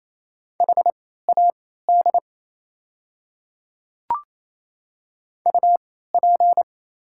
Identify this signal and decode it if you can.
Morse code, heard as a keyed tone, about 28 wpm, spaced as for 16 wpm, 700 Hz. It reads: HAD UP
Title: 3x Character Spacing - Top 300 Words - 28wpm